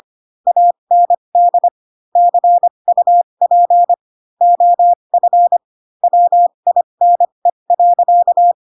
Morse code, heard as a keyed tone, about 25 wpm, spaced as for 22 wpm, 700 Hz. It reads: AND CUP OF WINE.
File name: Pinocchio / Pinocchio ch2 18 25 ik1hge